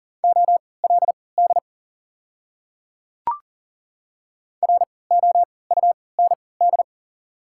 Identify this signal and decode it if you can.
Morse code, heard as a keyed tone, about 40 wpm, spaced as for 23 wpm, 700 Hz. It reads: OLD ROUND
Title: 3x Character Spacing - Top 200 Words - 40wpm